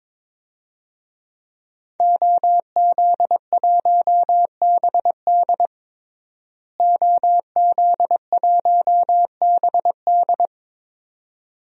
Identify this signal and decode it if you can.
Morse code, heard as a keyed tone, about 22 wpm, 700 Hz. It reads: OZ1BD OZ1BD
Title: DXCC-100 Call Signs - 5 Characters - Repeated 2x - 22wpm